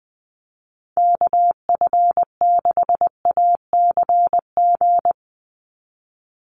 Morse code, heard as a keyed tone, about 20 wpm, 700 Hz. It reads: KF6ACG
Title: USA Call Signs 20wpm